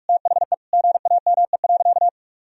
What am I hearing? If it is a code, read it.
Morse code, heard as a keyed tone, about 46 wpm, 700 Hz. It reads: THE GAME.
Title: Sentences from Top 1000 Words - Review Part 3 - 45wpm